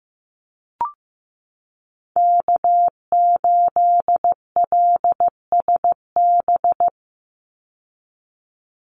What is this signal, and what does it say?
Morse code, heard as a keyed tone, about 15 wpm, 700 Hz. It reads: K8LSB
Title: Call Signs Repeated 3x - 15wpm